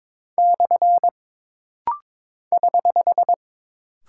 Morse code, heard as a keyed tone, about 22 wpm, 700 Hz. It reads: / <HH>
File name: practice / 22wpm / Single Punctuation-Prosign - Rapid-Fire - 22wpm